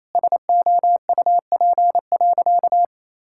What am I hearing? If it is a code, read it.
Morse code, heard as a keyed tone, about 28 wpm, 700 Hz. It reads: SOUP.
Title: Sentences from Top 1000 Words - Review Part 6 - 28wpm